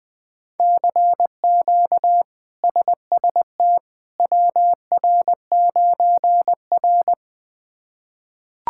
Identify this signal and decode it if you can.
Morse code, heard as a keyed tone, about 21 wpm, 700 Hz. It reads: CQ SST WR9R
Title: K1USN SST QSO Practice - 20wpm